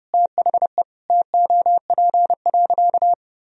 Morse code, heard as a keyed tone, about 30 wpm, 700 Hz. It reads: THE TOP.
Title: Sentences from Top 1000 Words - Review Part 2 - 30wpm